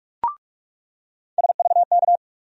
Morse code, heard as a keyed tone, about 43 wpm, 700 Hz. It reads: SVK
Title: USA Call Sign Postfixes - 3 Letters 45wpm